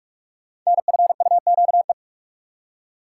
Morse code, heard as a keyed tone, about 44 wpm, 700 Hz. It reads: NFUQE
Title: Code Groups - 5 Letter:Numbers - 45wpm